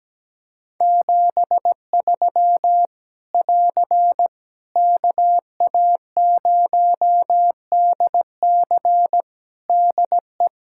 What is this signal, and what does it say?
Morse code, heard as a keyed tone, about 17 wpm, 700 Hz. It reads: 73 <AR> KA0DC DE
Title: Practice QSO - Exchange 4 - Ending - 17wpm